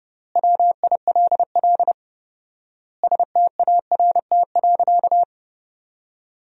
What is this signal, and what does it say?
Morse code, heard as a keyed tone, about 30 wpm, 700 Hz. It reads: WILL START.
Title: 4x Word-Spacing - Sentences from Top 300 Words - 30wpm